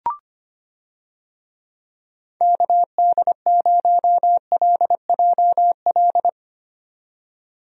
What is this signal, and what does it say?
Morse code, heard as a keyed tone, about 25 wpm, 700 Hz. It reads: KD0LJL
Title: USA Call Signs - 6 Characters - 25wpm